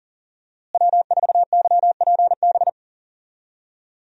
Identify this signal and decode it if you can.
Morse code, heard as a keyed tone, about 40 wpm, 700 Hz. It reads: W4YPB
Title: USA Call Signs - 5 Characters - Repeated 2x - 40wpm